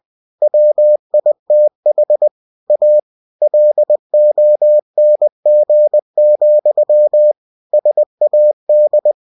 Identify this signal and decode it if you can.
Morse code, heard as a keyed tone, about 20 wpm, 600 Hz. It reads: WITH A LONG, SAD
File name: cw-books / winnie-the-pooh / winnie-20wpm-page-0027